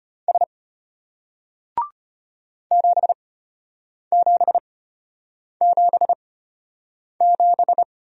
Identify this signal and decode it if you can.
Morse code, heard as a keyed tone, about 33 wpm, spaced as for 8 wpm, 700 Hz. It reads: S 7777